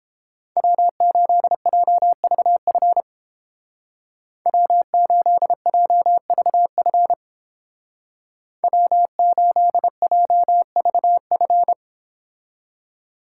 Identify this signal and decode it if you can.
Morse code, heard as a keyed tone, about 29 wpm, 700 Hz. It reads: W8JVF W8JVF W8JVF